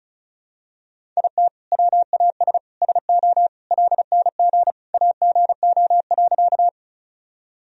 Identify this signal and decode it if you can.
Morse code, heard as a keyed tone, about 36 wpm, 700 Hz. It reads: IT WAS SO LNG AGO.
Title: Sentences from Top 500 Words and Abbreviations 35wpm